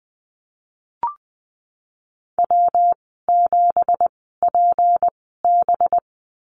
Morse code, heard as a keyed tone, about 19 wpm, spaced as for 14 wpm, 700 Hz. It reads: W7PB
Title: 213 Lesson 34g - Number 8 - Call Signs 20wpm